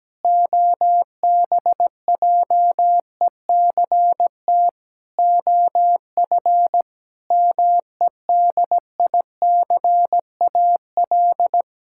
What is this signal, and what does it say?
Morse code, heard as a keyed tone, about 17 wpm, 700 Hz. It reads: OBJECT OF MEDICAL
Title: Sentences from Top 1000 Words - Review Part 6 - 17wpm